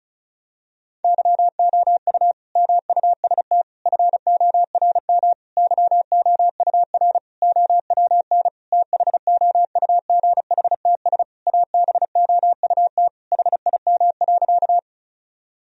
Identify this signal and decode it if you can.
Morse code, heard as a keyed tone, about 36 wpm, 700 Hz. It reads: YOU MUST FORM YOUR OWN THOUGHTS ABOUT HIM.
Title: Sentences from Top 1000 Words - Review Part 1 - 35wpm